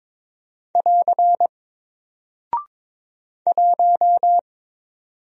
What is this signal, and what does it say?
Morse code, heard as a keyed tone, about 22 wpm, 700 Hz. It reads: <AR> 1
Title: Single Character (Letter:Number:Punctuation:Prosign) 22wpm